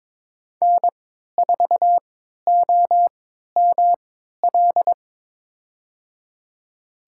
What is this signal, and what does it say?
Morse code, heard as a keyed tone, about 22 wpm, spaced as for 13 wpm, 700 Hz. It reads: N4OML
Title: USA Call Signs - 3x Character Spacing - 22wpm